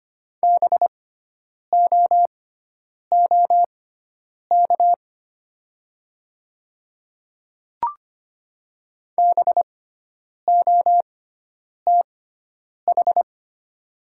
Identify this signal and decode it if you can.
Morse code, heard as a keyed tone, about 25 wpm, spaced as for 9 wpm, 700 Hz. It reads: BOOK BOTH
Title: 6x Character Spacing - Top 500 Words - 25wpm